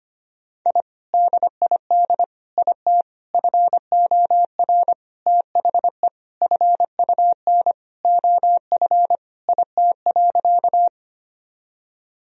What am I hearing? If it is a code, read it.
Morse code, heard as a keyed tone, about 25 wpm, 700 Hz. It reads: I DID IT FOR THE FUN OF IT.
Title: Sentences from Top 1000 Words - Review Part 4 - 25wpm